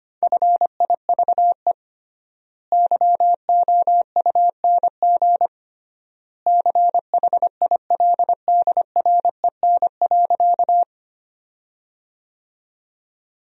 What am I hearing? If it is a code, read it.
Morse code, heard as a keyed tone, about 25 wpm, 700 Hz. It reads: FIVE YOUNG CHILDREN.
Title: 3x Word-Spacing - Sentences from Top 400 Words - 25wpm